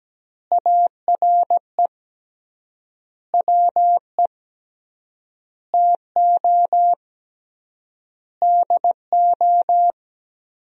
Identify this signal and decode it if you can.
Morse code, heard as a keyed tone, about 17 wpm, 700 Hz. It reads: ARE WE TO DO
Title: 3x Word-Spacing - Sentences from Top 100 Words - 17wpm